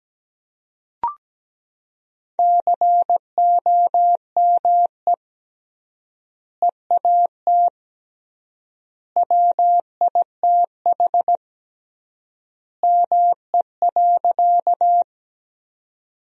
Morse code, heard as a keyed tone, about 17 wpm, 700 Hz. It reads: COME EAT WITH ME.
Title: 3x Word-Spacing - Sentences from Top 300 Words - 17wpm